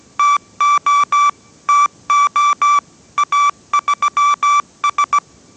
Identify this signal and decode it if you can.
Morse code, heard as a keyed tone, about 17 wpm, 1160 Hz. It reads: TO TO A3S
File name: 18wpm